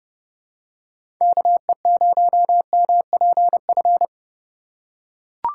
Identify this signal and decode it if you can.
Morse code, heard as a keyed tone, about 30 wpm, 700 Hz. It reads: KE0MPF
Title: USA Call Signs - 4x Character Spacing - 30wpm